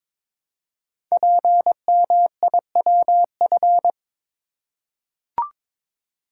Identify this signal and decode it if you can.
Morse code, heard as a keyed tone, about 22 wpm, 700 Hz. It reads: PMIWF
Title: Code Groups - 5 Letter:Numbers - 22wpm